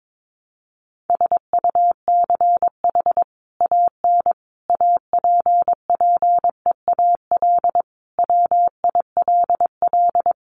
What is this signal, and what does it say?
Morse code, heard as a keyed tone, about 22 wpm, 700 Hz. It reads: SUCH AN APPEAL WILL
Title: Sentences from Top 1300 Words 22wpm